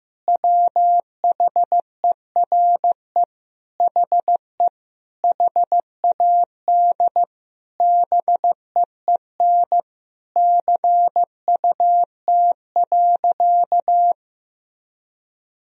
Morse code, heard as a keyed tone, about 15 wpm, 700 Hz. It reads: WHERE HE HAD BEEN CUT.
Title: Sentences from Top 1000 Words - Review Part 1 - 15wpm